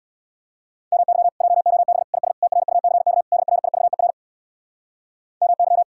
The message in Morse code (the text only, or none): MY JOB IS TRUCK DRIVER MY